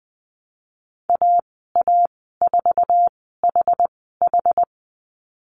Morse code, heard as a keyed tone, about 20 wpm, spaced as for 14 wpm, 700 Hz. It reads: AA4HH
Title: 069 Lesson 12g - Letter d - Call Signs 20wpm